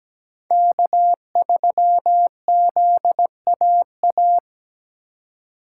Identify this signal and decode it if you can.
Morse code, heard as a keyed tone, about 18 wpm, 700 Hz. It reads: K3ZAA
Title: Speed Racing - USA 1x3 Call Signs 17wpm